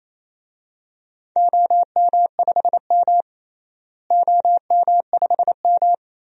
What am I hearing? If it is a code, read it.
Morse code, heard as a keyed tone, about 28 wpm, 700 Hz. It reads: OM5M OM5M
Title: DXCC-100 Call Signs - 4 Characters - Repeated 2x - 28wpm